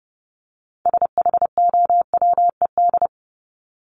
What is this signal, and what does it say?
Morse code, heard as a keyed tone, about 30 wpm, 700 Hz. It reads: SHOWED
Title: ED Suffix - Top 100 Words with Base Words 30wpm